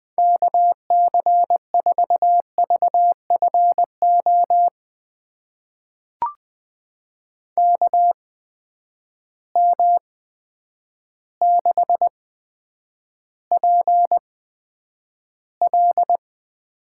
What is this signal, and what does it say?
Morse code, heard as a keyed tone, about 20 wpm, spaced as for 9 wpm, 700 Hz. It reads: KC4VFO KM6PL
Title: USA Call Signs - 8x Character Spacing - 20wpm